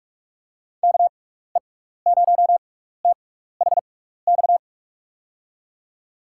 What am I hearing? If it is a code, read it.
Morse code, heard as a keyed tone, about 44 wpm, spaced as for 15 wpm, 700 Hz. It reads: KE0THX
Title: USA Call Signs - 6 Characters - 6x Character Spacing - 45wpm